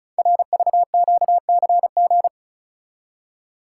Morse code, heard as a keyed tone, about 36 wpm, 700 Hz. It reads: RVQCG
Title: Code Groups - 5 Letters - 35wpm